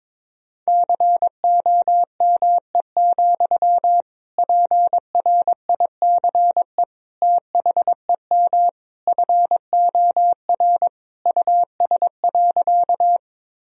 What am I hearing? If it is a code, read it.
Morse code, heard as a keyed tone, about 22 wpm, 700 Hz. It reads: COME, PRICE THEM FOR US.
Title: Sentences from Top 1000 Words - Review Part 4 - 22wpm